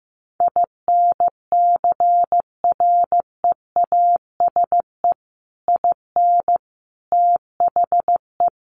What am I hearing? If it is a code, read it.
Morse code, heard as a keyed tone, about 14 wpm, 700 Hz. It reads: INCREASE IN THE
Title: Sets of 5 Words 15wpm